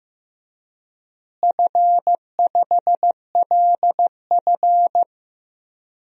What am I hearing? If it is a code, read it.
Morse code, heard as a keyed tone, about 15 wpm, 700 Hz. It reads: F5LF
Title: DXCC-100 Call Signs - 4 Characters - Repeated 3x - 15wpm